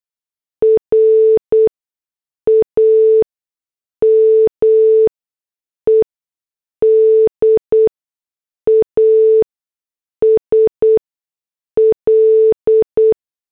Morse code, heard as a keyed tone, about 8 wpm, spaced as for 6 wpm, 440 Hz. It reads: RAMED AS L